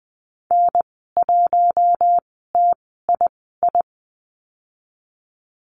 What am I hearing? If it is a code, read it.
Morse code, heard as a keyed tone, about 20 wpm, spaced as for 15 wpm, 700 Hz. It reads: N1TII
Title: 042 Lesson 8g - Number 1 - Call Signs 20wpm